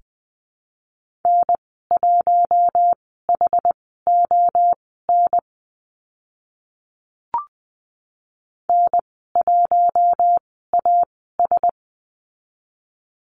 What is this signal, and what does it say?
Morse code, heard as a keyed tone, about 20 wpm, spaced as for 14 wpm, 700 Hz. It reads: N1HON N1AS
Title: 069 Lesson 12g - Letter d - Call Signs 20wpm